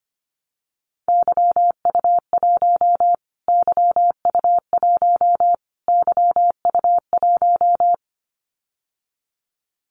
Morse code, heard as a keyed tone, about 25 wpm, 700 Hz. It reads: YU1 YU1 YU1
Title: Callsign Prefixes for Top 25 DXCC - Repeated 3x - 25wpm